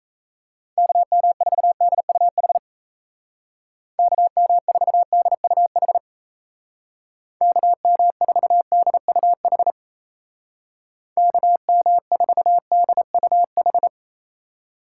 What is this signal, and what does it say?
Morse code, heard as a keyed tone, about 35 wpm, 700 Hz. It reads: KM4DUH KM4DUH KM4DUH KM4DUH
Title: Speed Racing - USA 2x3 Call Signs 28wpm